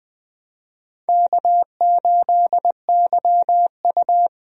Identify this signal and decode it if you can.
Morse code, heard as a keyed tone, about 21 wpm, 700 Hz. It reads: K8YU
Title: USA Call Signs - 4 Characters - Repeated 3x - 20wpm